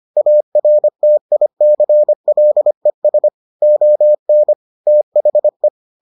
Morse code, heard as a keyed tone, about 25 wpm, 600 Hz. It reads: ARTICLES ON THE